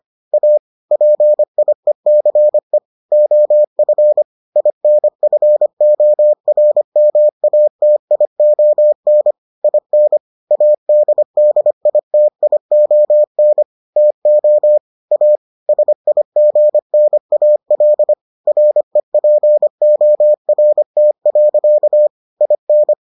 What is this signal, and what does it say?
Morse code, heard as a keyed tone, about 25 wpm, 600 Hz. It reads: A PIECE OF INFORMATION IN ADDITION TO A SIGNAL REPORT. IN